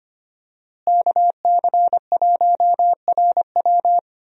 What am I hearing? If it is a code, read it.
Morse code, heard as a keyed tone, about 25 wpm, 700 Hz. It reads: KC1RW
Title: USA Call Signs - 5 Characters - Repeated 2x - 25wpm